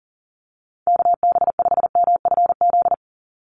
Morse code, heard as a keyed tone, about 40 wpm, 700 Hz. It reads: KB5MFZ
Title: USA Call Signs 40wpm